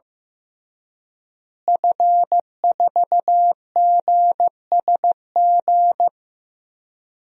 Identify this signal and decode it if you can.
Morse code, heard as a keyed tone, about 15 wpm, 700 Hz. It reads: F4GSG
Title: DXCC-100 Call Signs - 5 Characters - Repeated 3x - 15wpm